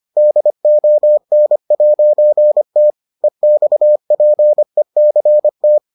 Morse code, heard as a keyed tone, about 25 wpm, 600 Hz. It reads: DON'T EXPECT